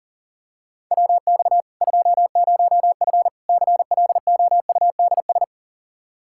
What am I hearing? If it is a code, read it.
Morse code, heard as a keyed tone, about 37 wpm, 700 Hz. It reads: WX 20F CLOUDS
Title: Practice QSO - Exchange 2 - Info - 40wpm